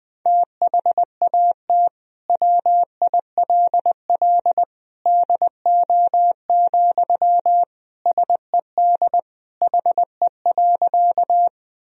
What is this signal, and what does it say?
Morse code, heard as a keyed tone, about 21 wpm, 700 Hz. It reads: THAT WILL DO, SED HE.
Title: Sentences from Top 100 Words and Abbreviations 20wpm